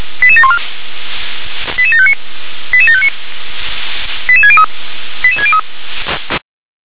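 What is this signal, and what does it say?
Five-tone selective call (ZVEI), heard as quick runs of five tones, tones of about 70 ms. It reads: 88714 88759 88759 88603 88603
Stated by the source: ZVEI five-tone selective call received by SDR.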